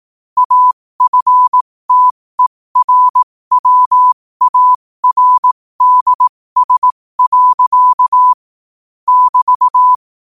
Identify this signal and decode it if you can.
Morse code, heard as a keyed tone, about 18 wpm, spaced as for 15 wpm, 1000 Hz. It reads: AFTERWARDS. <BT>
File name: qotdfiles / 15wpm